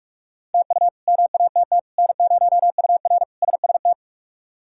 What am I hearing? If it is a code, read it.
Morse code, heard as a keyed tone, about 46 wpm, 700 Hz. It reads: TU MATT N0UR SST
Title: K1USN SST QSO Practice - 45wpm